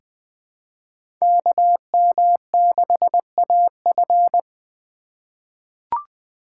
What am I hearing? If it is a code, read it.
Morse code, heard as a keyed tone, about 20 wpm, 700 Hz. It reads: KM6AF
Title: USA Call Signs - 5 Characters - 4x Character Spacing - 20wpm